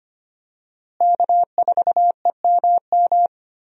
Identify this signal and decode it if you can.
Morse code, heard as a keyed tone, about 25 wpm, 700 Hz. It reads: K4EMM
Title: USA Call Signs - 5 Characters - Repeated 2x - 25wpm